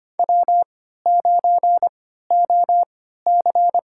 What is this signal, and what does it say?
Morse code, heard as a keyed tone, about 25 wpm, spaced as for 14 wpm, 700 Hz. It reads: W9OC
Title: USA Call Signs - 4 Characters - 3x Character Spacing - 25wpm